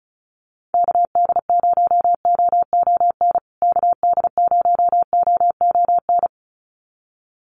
Morse code, heard as a keyed tone, about 34 wpm, 700 Hz. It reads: KD0OON KD0OON
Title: USA Call Signs Repeated 2x 35wpm